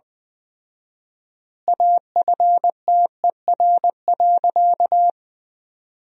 Morse code, heard as a keyed tone, about 20 wpm, 700 Hz. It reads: AFTER.
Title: 4x Word-Spacing - Sentences from Top 500 Words - 20wpm